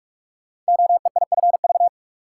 Morse code, heard as a keyed tone, about 44 wpm, 700 Hz. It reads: OEIFV
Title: Code Groups - 5 Letters - 45wpm